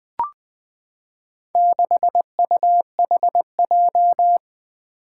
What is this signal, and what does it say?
Morse code, heard as a keyed tone, about 20 wpm, 700 Hz. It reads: 6UHJ
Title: Four Challenging Letter-Number Combinations 20wpm